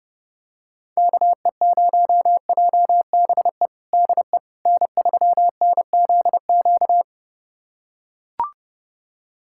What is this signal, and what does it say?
Morse code, heard as a keyed tone, about 30 wpm, 700 Hz. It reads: KE0JBE DE N3NZQ
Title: USA Callsign DE Callsign 30wpm